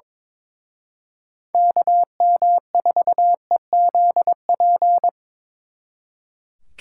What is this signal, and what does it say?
Morse code, heard as a keyed tone, about 26 wpm, 700 Hz. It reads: KM4EZP
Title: Speed Racing - USA 2x3 Call Signs 22wpm